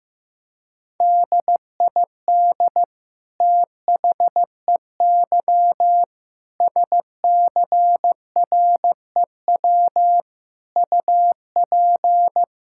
Morse code, heard as a keyed tone, about 15 wpm, 700 Hz. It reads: DID THEY SCREW UP
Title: Sentences from Top 1600 Words 15wpm